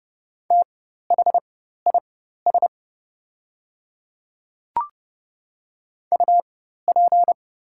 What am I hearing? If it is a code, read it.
Morse code, heard as a keyed tone, about 30 wpm, spaced as for 14 wpm, 700 Hz. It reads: THIS UP
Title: 4x Character Spacing - Top 300 Words - 30wpm